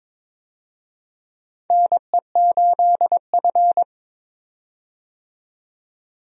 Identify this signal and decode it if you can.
Morse code, heard as a keyed tone, about 22 wpm, 700 Hz. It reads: NE8F
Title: USA Call Signs - 4 Characters - 22wpm